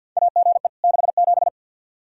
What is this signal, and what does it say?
Morse code, heard as a keyed tone, about 52 wpm, 700 Hz. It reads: AGE 67
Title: K1USN SST QSO Practice 50wpm